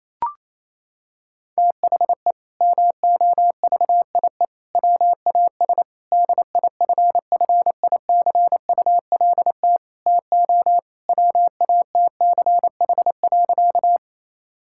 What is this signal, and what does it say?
Morse code, heard as a keyed tone, about 28 wpm, 700 Hz. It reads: THE MOVIE WAS DIFFICULT TO WATCH.
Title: Sentences from Top 1000 Words - Review Part 5 - 28wpm